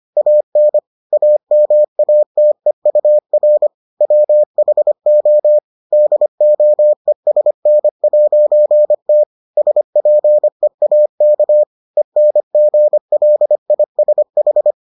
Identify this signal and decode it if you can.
Morse code, heard as a keyed tone, about 25 wpm, 600 Hz. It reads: AN AMATEUR WHO DOESN'T SPEAK ENGLISH